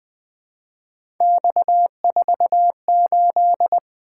Morse code, heard as a keyed tone, about 20 wpm, 700 Hz. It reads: X48
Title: Three Letter-Number Combinations - 20wpm